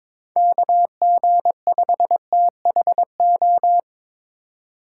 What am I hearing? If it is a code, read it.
Morse code, heard as a keyed tone, about 25 wpm, 700 Hz. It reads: KG5THO
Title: Speed Racing - USA 2x3 Call Signs 22wpm